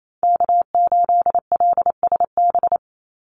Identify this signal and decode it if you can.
Morse code, heard as a keyed tone, about 28 wpm, 700 Hz. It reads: K8LSB
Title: Call Signs Repeated 3x - 28wpm